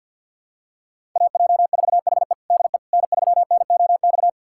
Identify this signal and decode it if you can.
Morse code, heard as a keyed tone, about 51 wpm, 700 Hz. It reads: AJ4HE DE N3NOX
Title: USA Callsign DE Callsign - 5 Characters - 50wpm